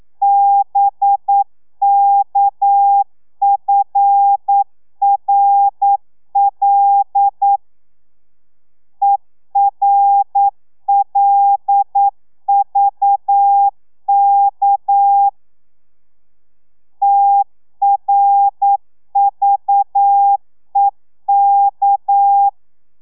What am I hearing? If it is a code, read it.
Morse code, heard as a keyed tone, about 9 wpm, 800 Hz. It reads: BKFRL ERLVK TRVEK
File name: telegrafia / i2tfj / Lez6 40